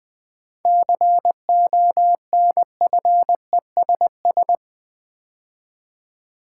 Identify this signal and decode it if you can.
Morse code, heard as a keyed tone, about 21 wpm, 700 Hz. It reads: CONFESS
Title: 7 Letter Words - First 3 Letters Given - 20wpm